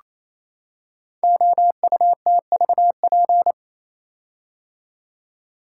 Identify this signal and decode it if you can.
Morse code, heard as a keyed tone, about 27 wpm, 700 Hz. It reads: OUTVP
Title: Code Groups - 5 Letter:Numbers - 28wpm